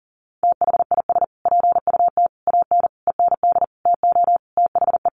Morse code, heard as a keyed tone, about 40 wpm, 700 Hz. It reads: THIS PUT AN END TO THE
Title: Sentences from Top 1100 Words 40wpm